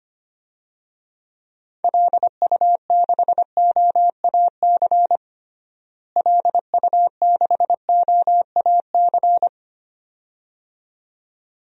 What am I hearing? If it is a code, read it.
Morse code, heard as a keyed tone, about 25 wpm, 700 Hz. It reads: LU6OAC LU6OAC
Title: DXCC-100 Call Signs - 6 Characters - Repeated 2x - 25wpm